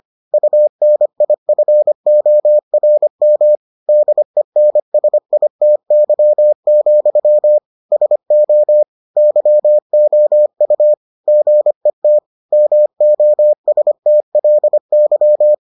Morse code, heard as a keyed tone, about 25 wpm, 600 Hz. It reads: UNIFORM DENSITY, SO YOU GET MOSTLY